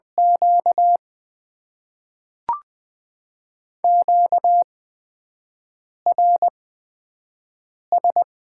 Morse code, heard as a keyed tone, about 20 wpm, spaced as for 5 wpm, 700 Hz. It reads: Q QRS
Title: Top 90 QSO Elements - 8x Character Spacing - 20wpm